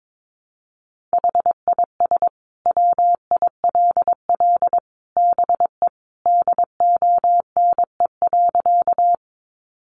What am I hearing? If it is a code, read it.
Morse code, heard as a keyed tone, about 22 wpm, 700 Hz. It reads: HIS WILL BE DONE.
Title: Sentences from Top 500 Words 22wpm